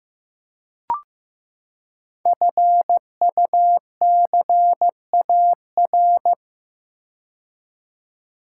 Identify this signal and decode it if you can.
Morse code, heard as a keyed tone, about 15 wpm, 700 Hz. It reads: FUCAR
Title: Code Groups - 5 Letter:Numbers - 15wpm